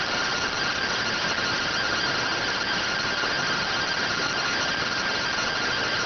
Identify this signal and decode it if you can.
Two-tone teleprinter signal, about 170 Hz shift.